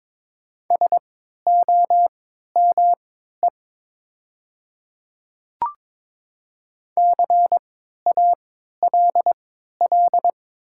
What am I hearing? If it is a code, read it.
Morse code, heard as a keyed tone, about 22 wpm, spaced as for 12 wpm, 700 Hz. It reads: SOME CALL
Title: 3x Character Spacing - Top 200 Words - 22wpm